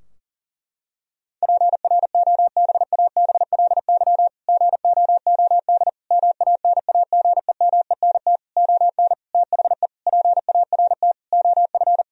Morse code, heard as a keyed tone, about 40 wpm, 700 Hz. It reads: PROBABLY GOOD MANAGEMENT ON THE PART OF
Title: Sentences from Top 1000 Words - Review Part 4 - 40wpm